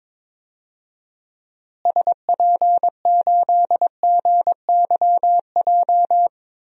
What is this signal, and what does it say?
Morse code, heard as a keyed tone, about 21 wpm, 700 Hz. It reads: SP8GYJ
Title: DXCC-100 Call Signs - 6 Characters - Repeated 2x - 22wpm